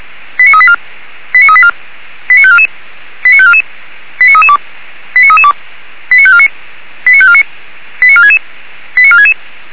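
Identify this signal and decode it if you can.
Five-tone selective call (ZVEI), heard as quick runs of five tones, tones of about 70 ms. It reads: 89384 89384 89544 89544 89332 89332 89549 89549 89466 89466
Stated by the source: ZVEI five-tone call received on an SDR.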